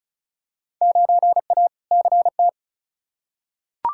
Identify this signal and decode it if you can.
Morse code, heard as a keyed tone, about 34 wpm, 700 Hz. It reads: 9A CT
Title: Field Day Exchanges Encore - 35wpm